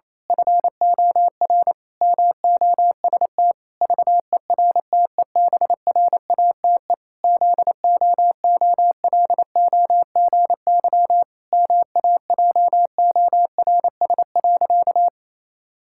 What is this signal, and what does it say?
Morse code, heard as a keyed tone, about 28 wpm, 700 Hz. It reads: FOR MOST VERTEBRATE ZOOLOGY MAJORS.
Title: Pangram Challenge v2 - 28wpm